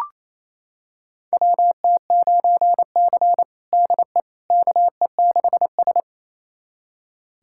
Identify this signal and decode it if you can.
Morse code, heard as a keyed tone, about 28 wpm, 700 Hz. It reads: WT9C DE KE6S
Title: USA Callsign DE Callsign - 4 Characters - 28wpm